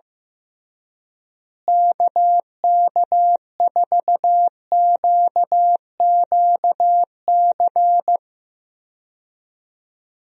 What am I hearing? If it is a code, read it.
Morse code, heard as a keyed tone, about 15 wpm, 700 Hz. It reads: KK4QQC
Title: USA Call Signs - 6 Characters - Repeated 3x - 15wpm